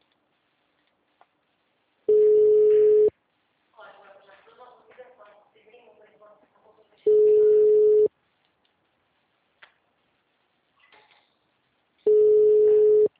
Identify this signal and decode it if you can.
Ringback tone (Europe, 425 Hz), with speech.
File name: out- -2007- - - .7099